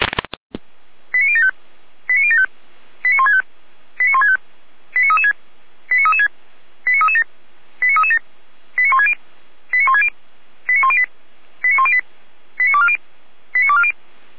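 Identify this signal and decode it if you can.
Five-tone selective call (ZVEI), heard as quick runs of five tones, tones of about 80 ms. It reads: 89075 89075 89165 89165 89336 89336 89337 89337 89160 89160 89108 89108 89240 89240
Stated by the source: ZVEI five-tone selective call 89072 received on SDR.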